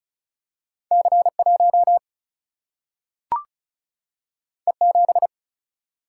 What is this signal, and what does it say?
Morse code, heard as a keyed tone, about 36 wpm, 700 Hz. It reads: C1 E7
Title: Single-Letter Single-Number 35wpm